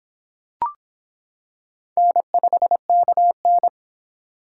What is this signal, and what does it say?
Morse code, heard as a keyed tone, about 25 wpm, 700 Hz. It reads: N5KN